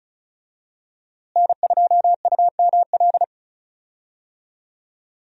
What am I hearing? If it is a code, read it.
Morse code, heard as a keyed tone, about 36 wpm, 700 Hz. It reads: N2UML
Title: USA Call Signs - 5 Characters - Repeated 3x - 35wpm